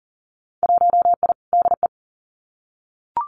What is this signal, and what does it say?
Morse code, heard as a keyed tone, about 38 wpm, 700 Hz. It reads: 1I DE
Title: Winter Field Day Exchanges Encore 40wpm